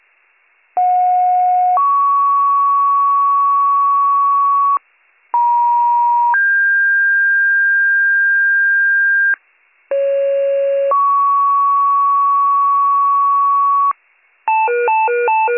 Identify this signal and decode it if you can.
Two-tone page, heard as a one-second tone, then a three-second tone, about 730 then 1120 Hz, about 950 then 1600 Hz, about 550 then 1120 Hz. It is heard from 0.8 to 13.9 s.